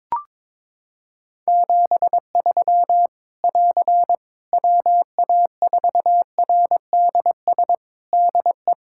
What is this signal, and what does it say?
Morse code, heard as a keyed tone, about 22 wpm, 700 Hz. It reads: 73 <AR> WA4RDS DE
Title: Practice QSO - Exchange 4 - Ending - 22wpm